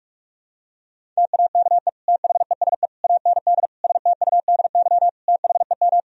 Morse code, heard as a keyed tone, about 43 wpm, 700 Hz. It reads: TAKE THESE AND STUDY THEM
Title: Sentences from Top 1000 Words - Review Part 1 - 45wpm